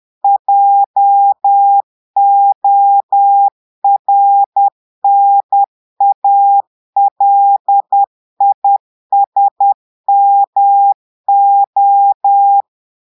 Morse code, heard as a keyed tone, about 10 wpm, 800 Hz. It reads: JORNALISMO
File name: Portugais - texte en clair 1 - vitesse - 10 mots minute